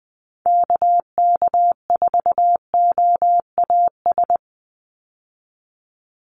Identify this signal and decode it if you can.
Morse code, heard as a keyed tone, about 20 wpm, 700 Hz. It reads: KK4OAS
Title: USA Call Signs 20wpm